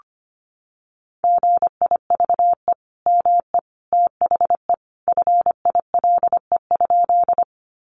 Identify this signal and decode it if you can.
Morse code, heard as a keyed tone, about 25 wpm, 700 Hz. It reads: GIVE ME THE FILE?
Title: Sentences from Top 1200 Words 25wpm